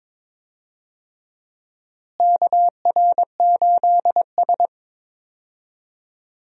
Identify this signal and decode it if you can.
Morse code, heard as a keyed tone, about 22 wpm, 700 Hz. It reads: KR8S
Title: USA Call Signs - 4 Characters - Repeated 3x - 22wpm